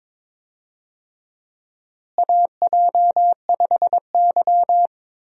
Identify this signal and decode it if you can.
Morse code, heard as a keyed tone, about 22 wpm, 700 Hz. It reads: AJ5Y
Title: USA Call Signs - 4 Characters - Repeated 3x - 22wpm